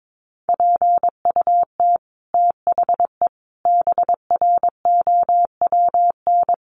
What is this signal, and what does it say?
Morse code, heard as a keyed tone, about 22 wpm, 700 Hz. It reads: PUT THE BROWN
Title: Sentences from Top 1200 Words 22wpm